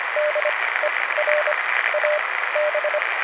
Morse code, heard as a keyed tone, about 25 wpm, spaced as for 16 wpm, 600 Hz, with radio noise. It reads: DER AB